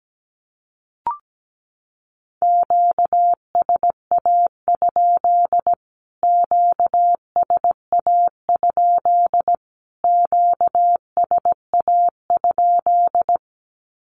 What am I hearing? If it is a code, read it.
Morse code, heard as a keyed tone, about 17 wpm, 700 Hz. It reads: QSA? QSA? QSA?
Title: Q-Codes Repeated 3x 17wpm